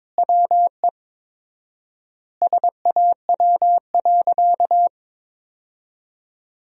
Morse code, heard as a keyed tone, about 22 wpm, 700 Hz. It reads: WE SAW.
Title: 4x Word-Spacing - Sentences from Top 300 Words - 22wpm